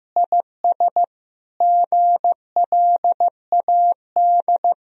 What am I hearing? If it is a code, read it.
Morse code, heard as a keyed tone, about 15 wpm, 700 Hz. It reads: IS GLAD
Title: Sentences from Top 1000 Words - Review Part 2 - 15wpm